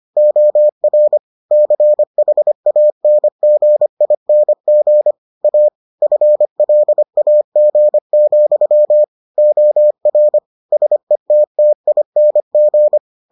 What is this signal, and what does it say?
Morse code, heard as a keyed tone, about 25 wpm, 600 Hz. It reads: OR CHANGING A FLAG, OR SETTING